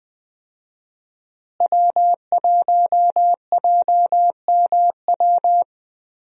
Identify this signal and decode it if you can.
Morse code, heard as a keyed tone, about 20 wpm, 700 Hz. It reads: W1JMW
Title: USA Call Signs - 5 Characters - Repeated 3x - 20wpm